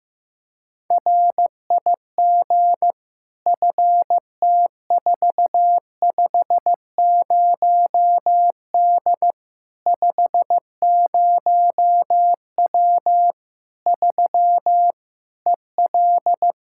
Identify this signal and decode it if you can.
Morse code, heard as a keyed tone, about 15 wpm, 700 Hz. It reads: RIG FT450D 50W 3 EL
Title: K1USN SST QSO Practice 15wpm